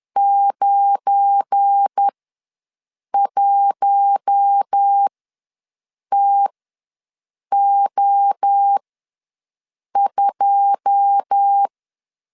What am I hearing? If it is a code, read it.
Morse code, heard as a keyed tone, about 10 wpm, spaced as for 6 wpm, 800 Hz. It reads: 91TO2
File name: LEZIONI / 9 2 ws 10wpm 4130